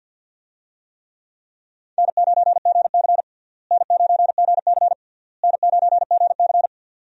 Morse code, heard as a keyed tone, about 51 wpm, 700 Hz. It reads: N9GC N9GC N9GC
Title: USA Call Signs - 4 Characters - Repeated 3x - 50wpm